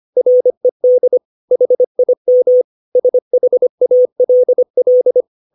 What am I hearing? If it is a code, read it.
Morse code, heard as a keyed tone, about 25 wpm, 500 Hz. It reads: RED HIM SHALL
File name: 632 words / 632 Words 25WPM